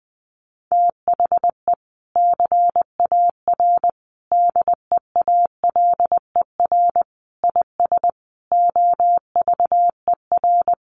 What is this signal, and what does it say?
Morse code, heard as a keyed tone, about 20 wpm, 700 Hz. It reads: THE CAR DEALER IS OVER
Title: Sentences from Top 1100 Words 20wpm